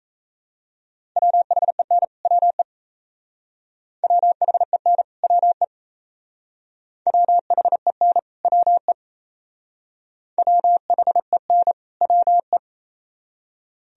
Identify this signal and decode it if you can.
Morse code, heard as a keyed tone, about 35 wpm, 700 Hz. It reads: WHEN WE WHEN WE WHEN WE WHEN WE
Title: Speed-Racing - Sets of 2 Words 28wpm